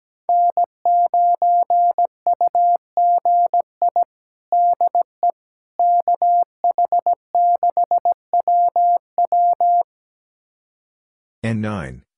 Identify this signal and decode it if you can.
Morse code, heard as a keyed tone, about 17 wpm, 700 Hz. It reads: N9UGI DE KH6WW
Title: USA Callsign DE Callsign - 5 Characters - 17wpm